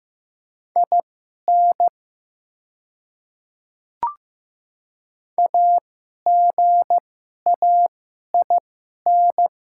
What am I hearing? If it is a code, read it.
Morse code, heard as a keyed tone, about 15 wpm, spaced as for 11 wpm, 700 Hz. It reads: IN AGAIN
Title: 2x Character Spacing - Top 200 Words - 15wpm